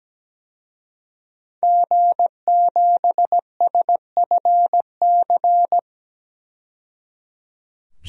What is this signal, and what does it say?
Morse code, heard as a keyed tone, about 17 wpm, 700 Hz. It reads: G7SFC
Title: DXCC-100 Call Signs - 5 Characters - 17wpm